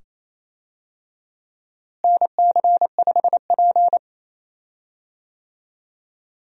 Morse code, heard as a keyed tone, about 28 wpm, 700 Hz. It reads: NC5P
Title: USA Call Signs - 4 Characters - Repeated 2x - 28wpm